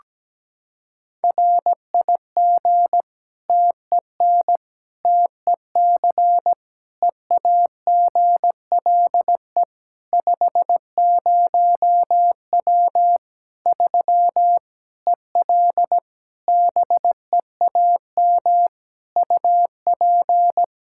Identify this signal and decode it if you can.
Morse code, heard as a keyed tone, about 17 wpm, 700 Hz. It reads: RIG TEN TEC EAGLE 50W 3 EL BEAM UP
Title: Practice QSO - Exchange 2 - Info - 17wpm